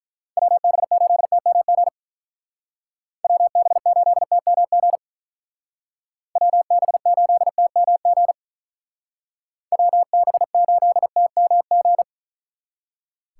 Morse code, heard as a keyed tone, about 43 wpm, 700 Hz. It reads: WB8TMG WB8TMG WB8TMG WB8TMG
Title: Speed Racing - USA 2x3 Call Signs 35wpm